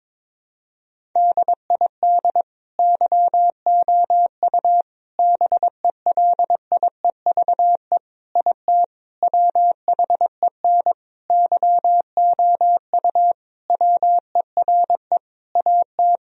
Morse code, heard as a keyed tone, about 22 wpm, 700 Hz. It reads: DID YOU BELIEVE IT WHEN YOU WERE AT
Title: Sentences from Top 1000 Words - Review Part 1 - 22wpm